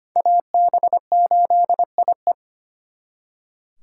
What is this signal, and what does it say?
Morse code, heard as a keyed tone, about 32 wpm, 700 Hz. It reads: AB8IE